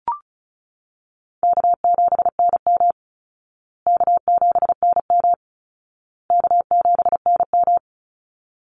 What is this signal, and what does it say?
Morse code, heard as a keyed tone, about 34 wpm, 700 Hz. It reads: K7NM K7NM K7NM
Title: Call Signs Repeated 2x - 35wpm